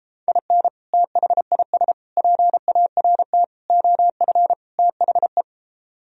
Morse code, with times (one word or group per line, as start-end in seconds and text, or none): 0.3-0.7 IN
0.9-1.9 THIS
2.2-3.4 PART
3.7-4.5 OF
4.8-5.4 THE